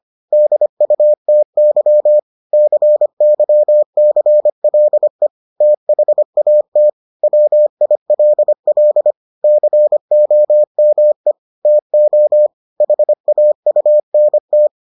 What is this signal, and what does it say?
Morse code, heard as a keyed tone, about 25 wpm, 600 Hz. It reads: DUTY CYCLE THAT WILL COME TO HAUNT